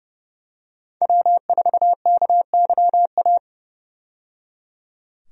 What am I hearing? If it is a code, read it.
Morse code, heard as a keyed tone, about 30 wpm, 700 Hz. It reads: W4KYA
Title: USA Call Signs - 5 Characters - 30wpm